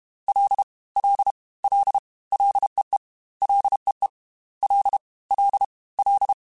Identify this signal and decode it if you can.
Morse code, heard as a keyed tone, about 32 wpm, 800 Hz. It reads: L L L LEE LEE L L L
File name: Ls / Single L m3s / Single L @30